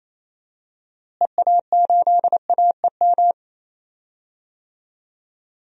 Morse code, heard as a keyed tone, about 28 wpm, 700 Hz. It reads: EA8AEM
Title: DXCC-100 Call Signs - 6 Characters - Repeated 3x - 28wpm